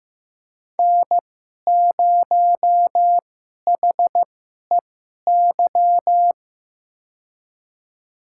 Morse code, heard as a keyed tone, about 15 wpm, spaced as for 11 wpm, 700 Hz. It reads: N0HEY
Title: USA Call Signs - 5 Characters - 2x Character Spacing - 15wpm